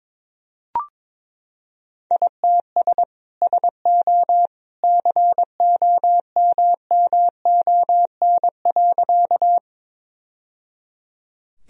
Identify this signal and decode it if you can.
Morse code, heard as a keyed tone, about 21 wpm, 700 Hz. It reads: ITS SO COMMON.
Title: Sentences from Top 1000 Words - Review Part 3 - 22wpm